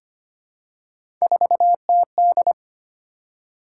Morse code, heard as a keyed tone, about 25 wpm, 700 Hz. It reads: 4TD
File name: Three Letter-Number Combinations - 25wpm